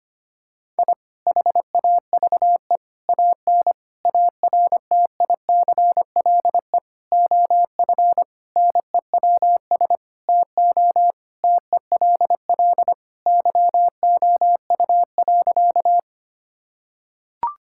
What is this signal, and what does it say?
Morse code, heard as a keyed tone, about 25 wpm, 700 Hz. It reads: I HAVE AN ARTICLE OF NEWS TO TELL YOU.
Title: Sentences from Top 1000 Words - Review Part 5 - 25wpm